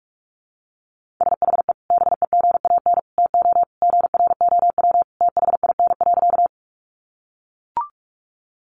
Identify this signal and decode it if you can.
Morse code, heard as a keyed tone, about 44 wpm, 700 Hz. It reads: SHE BEGAN TO GROW THIN.
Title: Sentences from Top 1200 Words 45wpm